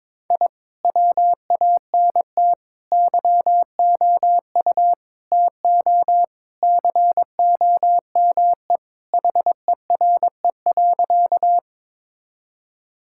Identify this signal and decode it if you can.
Morse code, heard as a keyed tone, about 22 wpm, 700 Hz. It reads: I WANT YOU TO COME HERE.
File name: practice / 22wpm / Sentences with 2-Word Phrases - 22wpm